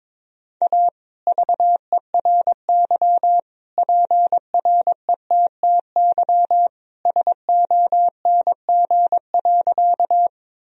Morse code, heard as a keyed tone, about 21 wpm, 700 Hz. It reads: A VERY PRETTY SONG.
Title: Sentences from Top 1000 Words - Review Part 2 - 22wpm